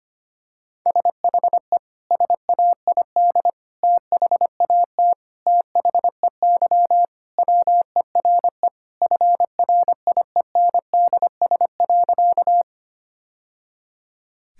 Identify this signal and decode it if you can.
Morse code, heard as a keyed tone, about 25 wpm, 700 Hz. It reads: SHE SAID THAT THEY WERE FRIENDS.
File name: practice / 25wpm / Sentences with 3-Word Phrases - 25wpm